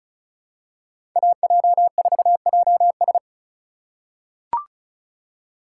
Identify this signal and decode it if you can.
Morse code, heard as a keyed tone, about 34 wpm, 700 Hz. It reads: AJ4JS
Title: USA Call Signs - 5 Characters - 3x Character Spacing - 35wpm